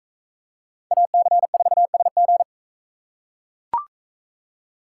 Morse code, heard as a keyed tone, about 39 wpm, 700 Hz. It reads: AC4SG